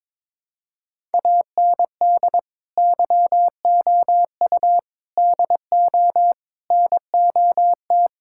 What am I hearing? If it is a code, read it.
Morse code, heard as a keyed tone, about 22 wpm, 700 Hz. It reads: AND YOU DO NOT
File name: practice / 22wpm / Sentences with 4-Word Phrases - 22wpm